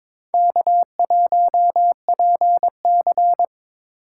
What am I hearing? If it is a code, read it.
Morse code, heard as a keyed tone, about 22 wpm, 700 Hz. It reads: K1PC
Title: USA Call Signs - 4 Characters - 4x Character Spacing - 22wpm